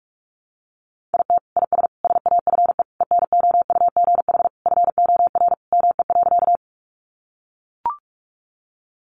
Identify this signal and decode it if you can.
Morse code, heard as a keyed tone, about 44 wpm, 700 Hz. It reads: IT IS SAFE ENOUGH FOR ME.
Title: Sentences from Top 900 Words 45wpm